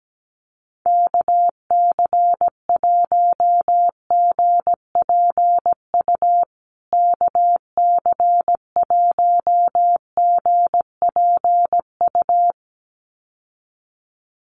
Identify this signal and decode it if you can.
Morse code, heard as a keyed tone, about 17 wpm, 700 Hz. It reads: KC1GPU KC1GPU
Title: USA Call Signs Repeated 2x 17wpm